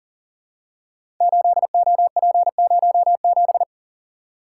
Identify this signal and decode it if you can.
Morse code, heard as a keyed tone, about 40 wpm, 700 Hz. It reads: 8OP07
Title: Code Groups - 5 Letter:Numbers - 40wpm